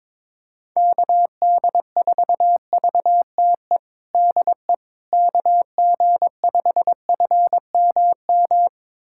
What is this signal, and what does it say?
Morse code, heard as a keyed tone, about 22 wpm, 700 Hz. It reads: KD4VTE DE KG5FMM
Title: USA Callsign DE Callsign - 6 Characters - 22wpm